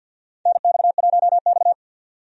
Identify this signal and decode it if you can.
Morse code, heard as a keyed tone, about 52 wpm, 700 Hz. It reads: NX1X
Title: USA Call Signs - 4 Characters - 50wpm